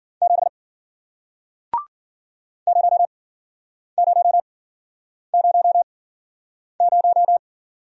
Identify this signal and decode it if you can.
Morse code, heard as a keyed tone, about 52 wpm, spaced as for 9 wpm, 700 Hz. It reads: 7 0000